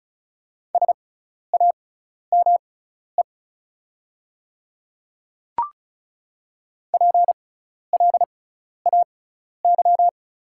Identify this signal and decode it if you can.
Morse code, heard as a keyed tone, about 33 wpm, spaced as for 12 wpm, 690 Hz. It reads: SAME PLAY
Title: 6x Character Spacing - Top 200 Words - 35wpm